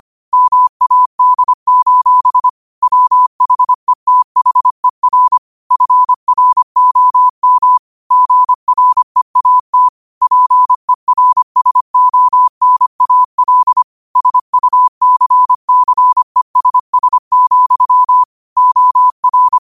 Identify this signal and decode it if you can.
Morse code, heard as a keyed tone, about 25 wpm, 1000 Hz. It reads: MAD: WHETHER FROM GREAT PERSONAL SUCCESS, OR
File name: qotdfiles / 25wpm